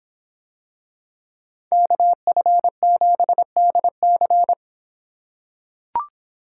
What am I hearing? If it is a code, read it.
Morse code, heard as a keyed tone, about 26 wpm, 700 Hz. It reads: KF7DC